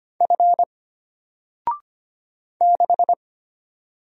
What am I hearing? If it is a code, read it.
Morse code, heard as a keyed tone, about 25 wpm, 700 Hz. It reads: F 6
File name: Single-Letter-Number-25wpm